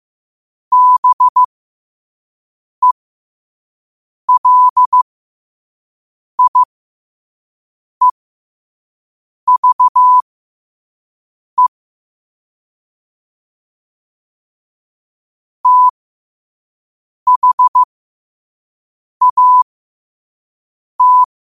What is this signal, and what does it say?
Morse code, heard as a keyed tone, about 15 wpm, spaced as for 5 wpm, 1000 Hz. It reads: BELIEVE THAT
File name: qotdfiles / 5wpm